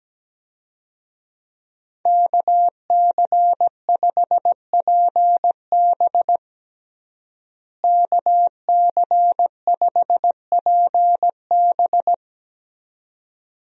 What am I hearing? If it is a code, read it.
Morse code, heard as a keyed tone, about 17 wpm, 700 Hz. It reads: KC5PB KC5PB
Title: USA Call Signs - 5 Characters - Repeated 3x - 17wpm